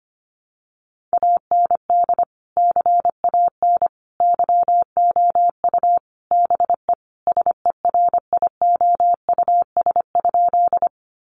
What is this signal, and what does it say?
Morse code, heard as a keyed tone, about 25 wpm, 700 Hz. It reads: AND CAN YOU BE SERIOUS?
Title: Sentences from Top 1100 Words 25wpm